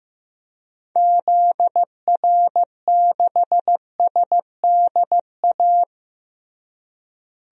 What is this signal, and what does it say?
Morse code, heard as a keyed tone, about 15 wpm, 700 Hz. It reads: ZR6SDA
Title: DXCC-100 Call Signs - 6 Characters - Repeated 3x - 15wpm